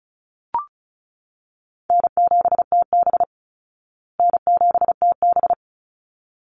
Morse code, heard as a keyed tone, about 34 wpm, 700 Hz. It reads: N7TB N7TB
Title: Easy Call Signs Repeated 2x - 35wpm